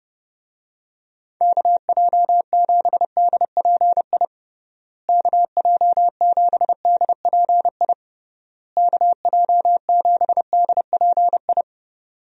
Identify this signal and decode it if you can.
Morse code, heard as a keyed tone, about 30 wpm, 700 Hz. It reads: KJ7DPI KJ7DPI KJ7DPI
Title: USA Call Signs - 6 Characters - Repeated 3x - 30wpm